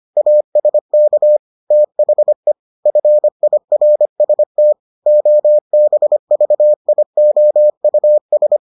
Morse code, heard as a keyed tone, about 25 wpm, 600 Hz. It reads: ASK THE FIRST OBVIOUS